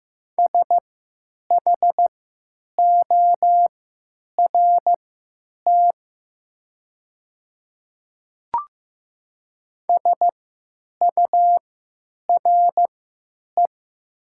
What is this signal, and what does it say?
Morse code, heard as a keyed tone, about 15 wpm, spaced as for 9 wpm, 700 Hz. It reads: SHORT SURE
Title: 3x Character Spacing - Top 500 Words - 15wpm